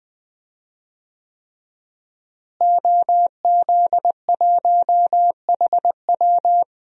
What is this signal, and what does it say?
Morse code, heard as a keyed tone, about 20 wpm, 700 Hz. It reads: OZ1HW
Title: DXCC-100 Call Signs - 5 Characters - Repeated 3x - 20wpm